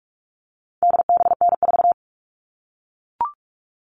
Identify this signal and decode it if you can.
Morse code, heard as a keyed tone, about 44 wpm, 700 Hz. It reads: DBN4
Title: All Cut Numbers - 4-Digit 45wpm